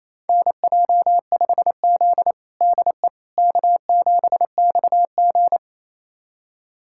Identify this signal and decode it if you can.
Morse code, heard as a keyed tone, about 28 wpm, 700 Hz. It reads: NJ5Z DE K7XG
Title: USA Callsign DE Callsign - 4 Characters - 28wpm